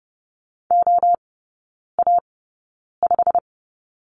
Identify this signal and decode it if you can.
Morse code, heard as a keyed tone, about 30 wpm, spaced as for 9 wpm, 700 Hz. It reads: OA5
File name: Lesson 15e - Number 5 - Three Characters 30wpm